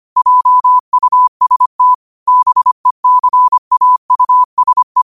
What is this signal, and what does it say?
Morse code, heard as a keyed tone, about 25 wpm, 1000 Hz. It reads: JUST BECAUSE